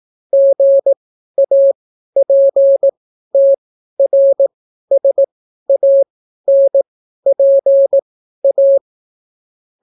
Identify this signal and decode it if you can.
Morse code, heard as a keyed tone, about 18 wpm, spaced as for 12 wpm, 550 Hz. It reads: GAPTRSANPA